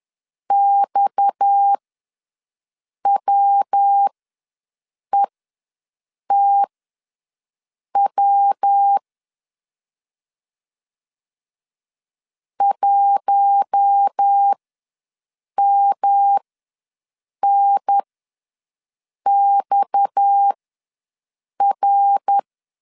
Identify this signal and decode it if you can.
Morse code, heard as a keyed tone, about 10 wpm, spaced as for 6 wpm, 800 Hz. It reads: XWETW 1MNXR
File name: LEZIONI / 7 GF9Z / 7 G ws 10wpm 3843